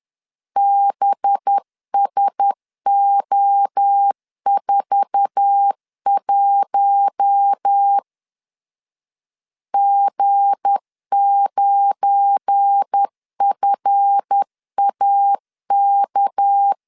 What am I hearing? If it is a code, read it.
Morse code, heard as a keyed tone, about 10 wpm, 800 Hz. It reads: BSO41 G9FAK
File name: LEZIONI / 8 YJ45 / 8 YJ4 10wpm 2633